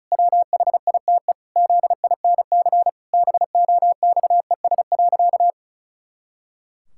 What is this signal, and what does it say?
Morse code, heard as a keyed tone, about 36 wpm, 700 Hz. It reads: WHITE ZINC BOXES.
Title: Pangram Challenge v2 - 35wpm